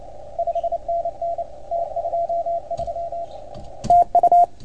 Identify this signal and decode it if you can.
Morse code, heard as a keyed tone, about 29 wpm, 690 Hz, with radio noise. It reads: 5NN T23 TU
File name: Archiv / CUC310 / OK1FGD CUC 310 pileup